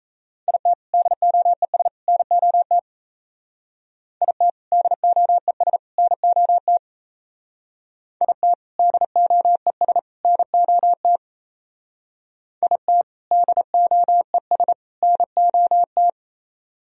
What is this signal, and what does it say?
Morse code, heard as a keyed tone, about 34 wpm, 700 Hz. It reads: IT DOES NOT IT DOES NOT IT DOES NOT IT DOES NOT
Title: Speed-Racing - Sets of 3 Words 28wpm